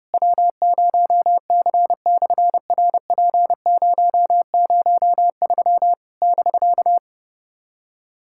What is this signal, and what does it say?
Morse code, heard as a keyed tone, about 30 wpm, 700 Hz. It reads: W0C/RP003 <BK>
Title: SOTA QSO Practice 30wpm